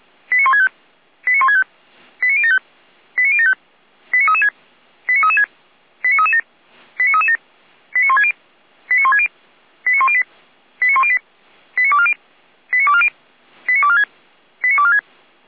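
Five-tone selective call (ZVEI), heard as quick runs of five tones, tones of about 80 ms. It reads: 89165 89165 89075 89075 89336 89336 89337 89337 89160 89160 89108 89108 89240 89240 89246 89246
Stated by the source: ZVEI five-tone selective call 89150 received by SDR.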